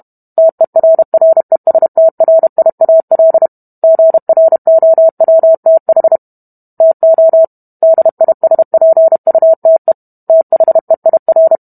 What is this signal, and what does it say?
Morse code, heard as a keyed tone, about 32 wpm, 650 Hz. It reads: TERRESTRIAL GROWTH TO DISPUTE THEIR